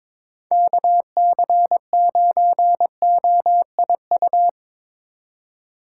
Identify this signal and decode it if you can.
Morse code, heard as a keyed tone, about 23 wpm, 700 Hz. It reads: KC9OIU
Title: USA Call Signs - 6 Characters - 4x Character Spacing - 22wpm